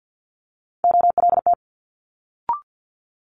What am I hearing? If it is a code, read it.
Morse code, heard as a keyed tone, about 50 wpm, 700 Hz. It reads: OLT
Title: Multi-letter Phonemes Quick Fire 50wpm